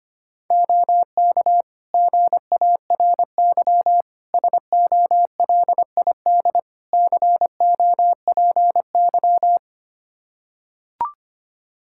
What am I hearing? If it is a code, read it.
Morse code, heard as a keyed tone, about 25 wpm, 700 Hz. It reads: OK GARY SOLID COPY
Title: Practice QSO - Exchange 3 - More Info - 25wpm